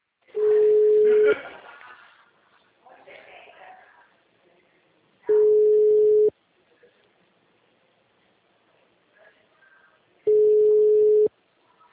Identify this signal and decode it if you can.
Ringback tone (Europe, 425 Hz), with speech.
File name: out- -2010- - - .63354